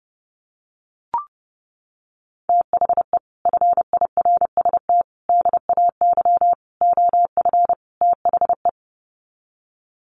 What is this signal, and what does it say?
Morse code, heard as a keyed tone, about 30 wpm, 700 Hz. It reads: THE FIRST DAY OF THE
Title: Sets of 5 Words Encore2 30wpm